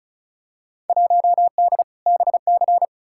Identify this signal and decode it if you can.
Morse code, heard as a keyed tone, about 36 wpm, 700 Hz. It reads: 1D BC
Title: Field Day Exchanges - 35wpm